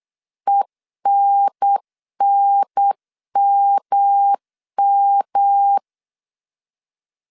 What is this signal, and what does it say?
Morse code, heard as a keyed tone, about 8 wpm, 800 Hz. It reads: ENNMM
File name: CW / DET / 1 9 1500C S B 8wpm 5624